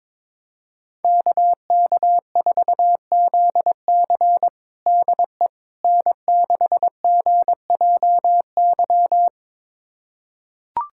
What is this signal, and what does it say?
Morse code, heard as a keyed tone, about 23 wpm, 700 Hz. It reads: KK4ZC DE N6GJY
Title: USA Callsign DE Callsign - 5 Characters - 22wpm